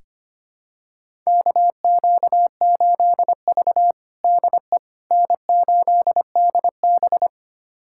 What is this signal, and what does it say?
Morse code, heard as a keyed tone, about 25 wpm, 700 Hz. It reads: KQ8V DE N8DB
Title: USA Callsign DE Callsign - 4 Characters - 25wpm